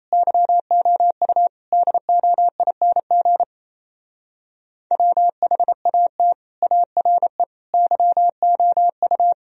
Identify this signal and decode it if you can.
Morse code, heard as a keyed tone, about 29 wpm, 700 Hz. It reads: YOU DOING WHAT ARE YOU
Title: Speed-Racing - Sets of 4 Words 28wpm